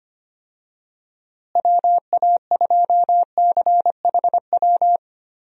Morse code, heard as a keyed tone, about 25 wpm, 700 Hz. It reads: WA2CHW
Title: USA Call Signs - 6 Characters - Repeated 3x - 25wpm